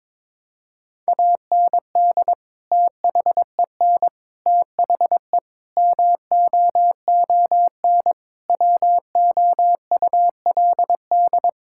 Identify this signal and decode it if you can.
Morse code, heard as a keyed tone, about 22 wpm, 700 Hz. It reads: AND THEN THE MOON WOULD
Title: Sentences from Top 1000 Words - Review Part 6 - 22wpm